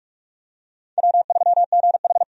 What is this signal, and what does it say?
Morse code, heard as a keyed tone, about 42 wpm, 700 Hz. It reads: W3GH
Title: USA Call Signs - 4 Characters - 45wpm